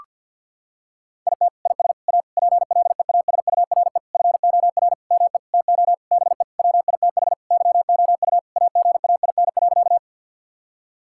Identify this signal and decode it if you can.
Morse code, heard as a keyed tone, about 50 wpm, 700 Hz. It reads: IT IS A PLEASURE FOR ME TO BE WITH YOU AGAIN.
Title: Sentences from Top 1600 Words 50wpm